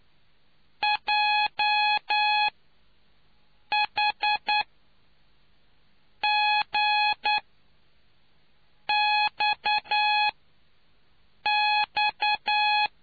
Morse code, heard as a keyed tone, about 9 wpm, spaced as for 5 wpm, 840 Hz. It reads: JHGXX